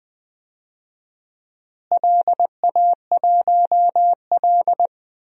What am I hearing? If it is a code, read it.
Morse code, heard as a keyed tone, about 20 wpm, 700 Hz. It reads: LA1L
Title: DXCC-100 Call Signs - 4 Characters - Repeated 3x - 20wpm